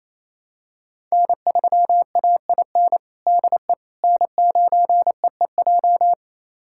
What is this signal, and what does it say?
Morse code, heard as a keyed tone, about 28 wpm, 700 Hz. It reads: N3AIN DE N9EEJ
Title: USA Callsign DE Callsign 28wpm